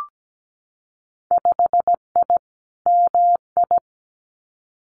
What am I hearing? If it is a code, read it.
Morse code, heard as a keyed tone, about 17 wpm, 700 Hz. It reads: 5I MI
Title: Winter Field Day Exchanges Encore 17wpm